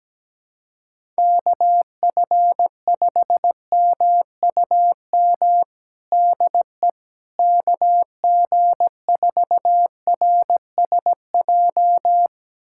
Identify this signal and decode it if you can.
Morse code, heard as a keyed tone, about 17 wpm, 700 Hz. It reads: KF5MUM DE KG4RSJ
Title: USA Callsign DE Callsign - 6 Characters - 17wpm